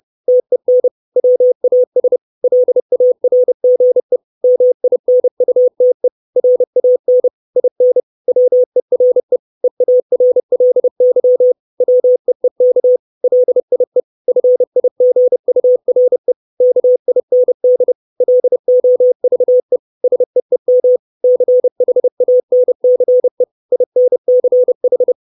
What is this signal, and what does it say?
Morse code, heard as a keyed tone, about 30 wpm, 500 Hz. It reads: TEN WAS LARGE MINUTE RAN IN WERE EARLY WEEK LIE FIGURE KIND LOVE SEEM CHANCE INCH
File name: 632 Words 30WPM